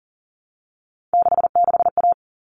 Morse code, heard as a keyed tone, about 40 wpm, 700 Hz. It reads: 66A
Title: All Cut Numbers - 3-Digit 40wpm